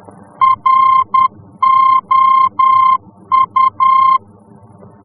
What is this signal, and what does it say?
Morse code, heard as a keyed tone, about 10 wpm, 1050 Hz. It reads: ROU
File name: france / mx / ROU-F-VOR